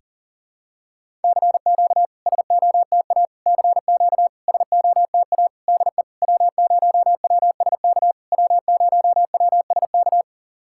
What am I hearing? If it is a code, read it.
Morse code, heard as a keyed tone, about 40 wpm, 700 Hz. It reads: CQ SOTA CQ SOTA DE W0WSK W0WSK
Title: SOTA QSO Practice 40wpm